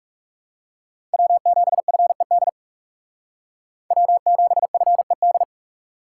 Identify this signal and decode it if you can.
Morse code, heard as a keyed tone, about 36 wpm, 700 Hz. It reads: W7FED W7FED
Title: Speed Racing - USA 1x3 Call Signs 30wpm